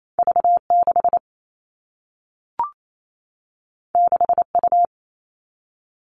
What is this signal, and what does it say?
Morse code, heard as a keyed tone, about 28 wpm, 700 Hz. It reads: V6 6U
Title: All Cut Numbers - 2-Digit 28wpm